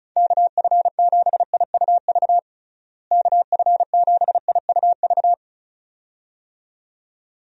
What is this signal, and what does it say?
Morse code, heard as a keyed tone, about 34 wpm, 700 Hz. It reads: KF7IUV KF7IUV
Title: USA Call Signs - 6 Characters - Repeated 2x - 35wpm